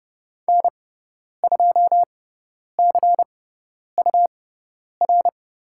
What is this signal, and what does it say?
Morse code, heard as a keyed tone, about 30 wpm, spaced as for 10 wpm, 700 Hz. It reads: N2CUR
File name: Lesson 17g - Letter u - Call Signs 30wpm